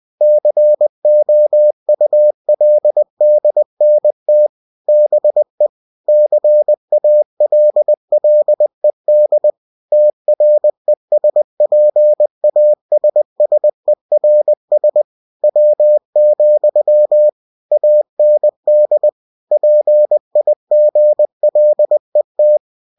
Morse code, heard as a keyed tone, about 20 wpm, 600 Hz. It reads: COULDNT BE CALLED TRESPASSERS W, AND PIGLET
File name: cw-books / winnie-the-pooh / winnie-20wpm-page-0020